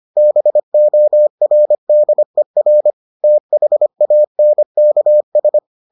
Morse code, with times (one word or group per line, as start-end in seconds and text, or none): 0.2-2.9 BORDER
3.2-5.6 THANKS